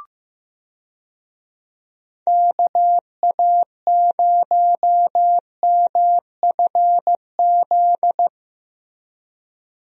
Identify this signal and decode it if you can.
Morse code, heard as a keyed tone, about 15 wpm, 700 Hz. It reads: KA0MFZ
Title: USA Call Signs - 6 Characters - Repeated 3x - 15wpm